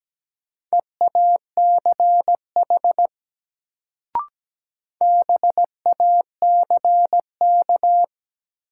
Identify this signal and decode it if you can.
Morse code, heard as a keyed tone, about 17 wpm, 700 Hz. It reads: EACH BACK
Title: Sentences from Top 200 Words - New Word Drill - 17wpm